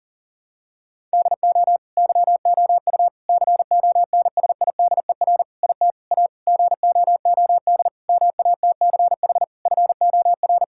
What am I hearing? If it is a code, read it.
Morse code, heard as a keyed tone, about 40 wpm, 700 Hz. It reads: DO YOU CONSIDER IT A GOOD MATCH FOR
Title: Sentences from Top 1600 Words 40wpm